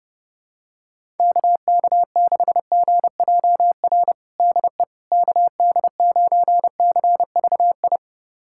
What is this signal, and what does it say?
Morse code, heard as a keyed tone, about 32 wpm, 700 Hz. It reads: KK6GJR DE KD9CVI
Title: USA Callsign DE Callsign - 6 Characters - 30wpm